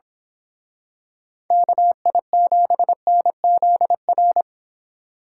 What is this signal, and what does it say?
Morse code, heard as a keyed tone, about 26 wpm, 700 Hz. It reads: KI7NZR
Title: Speed Racing - USA 2x3 Call Signs 22wpm